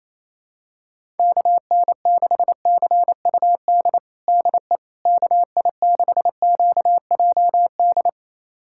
Morse code, heard as a keyed tone, about 27 wpm, 700 Hz. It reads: KN6CUD DE KI6QJD
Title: USA Callsign DE Callsign - 6 Characters - 28wpm